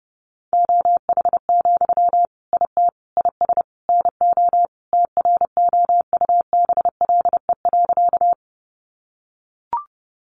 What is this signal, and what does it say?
Morse code, heard as a keyed tone, about 30 wpm, 700 Hz. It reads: OH, IT IS NO TROUBLE.
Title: Sentences from Top 1200 Words 30wpm